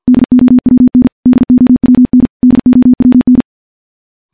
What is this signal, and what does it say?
Morse code, heard as a keyed tone, about 51 wpm, 250 Hz. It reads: DOWN DOWN DOWN
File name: morse / morse-training-50WPM